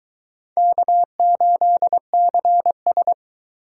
Morse code, heard as a keyed tone, about 20 wpm, 700 Hz. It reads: K8CS